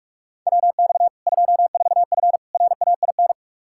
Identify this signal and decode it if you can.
Morse code, heard as a keyed tone, about 46 wpm, 700 Hz. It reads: WX 24F RAIN
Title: K1USN SST QSO Practice 45wpm